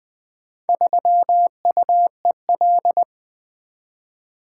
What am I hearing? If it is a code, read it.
Morse code, heard as a keyed tone, about 20 wpm, 700 Hz. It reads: 3UEL
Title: Four Letter-Number Combinations - 20wpm